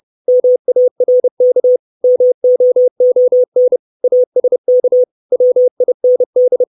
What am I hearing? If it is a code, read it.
Morse code, heard as a keyed tone, about 30 wpm, 500 Hz. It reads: MARK MOON ASK WIND
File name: 632 Words 30WPM